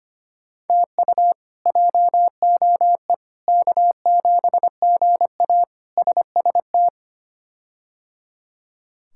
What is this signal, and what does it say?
Morse code, heard as a keyed tone, about 25 wpm, 700 Hz. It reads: TU JOE K7GA SST
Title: K1USN SST QSO Practice - 25wpm